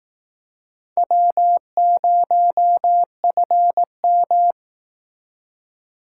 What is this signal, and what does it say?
Morse code, heard as a keyed tone, about 19 wpm, 700 Hz. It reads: W0FM
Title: Speed Racing - USA 1x2 Call Signs 15wpm